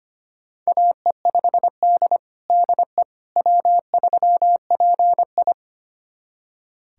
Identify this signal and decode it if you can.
Morse code, heard as a keyed tone, about 25 wpm, 700 Hz. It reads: AE5D DE W3PI
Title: USA Callsign DE Callsign - 4 Characters - 25wpm